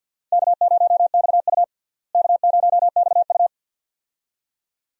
Morse code, heard as a keyed tone, about 46 wpm, 700 Hz. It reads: K0XU K0XU
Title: USA Call Signs - 4 Characters - Repeated 2x - 50wpm